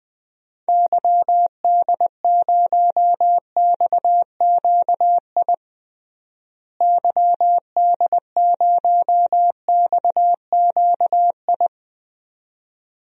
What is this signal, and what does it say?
Morse code, heard as a keyed tone, about 20 wpm, 700 Hz. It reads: YD0XQI YD0XQI
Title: DXCC-100 Call Signs - 6 Characters - Repeated 2x - 20wpm